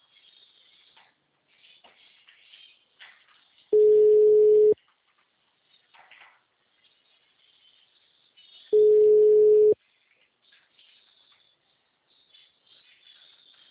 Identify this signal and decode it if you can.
Ringback tone (Europe, 425 Hz).
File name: out- -2039- - - .0